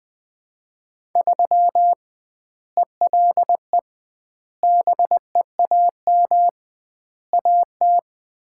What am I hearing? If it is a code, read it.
Morse code, heard as a keyed tone, about 21 wpm, 700 Hz. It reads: 3 ELE BEAM AT
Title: Common QSO Phrases - 2x Word-Spacing - 20wpm